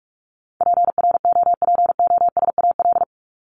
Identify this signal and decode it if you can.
Morse code, heard as a keyed tone, about 44 wpm, 700 Hz. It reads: PROPOSAL
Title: Top 1751-2000 Words 45wpm